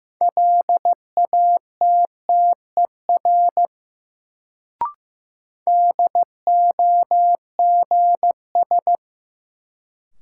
Morse code, heard as a keyed tone, about 15 wpm, 700 Hz. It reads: LATTER DOGS
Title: Sentences from Top 1600 Words - New Word Drill Alt - 15wpm